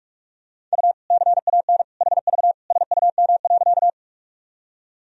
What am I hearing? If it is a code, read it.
Morse code, heard as a keyed tone, about 46 wpm, 700 Hz. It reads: U CAN HV SUM.
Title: Sentences from Top 100 Words and Abbreviations 45wpm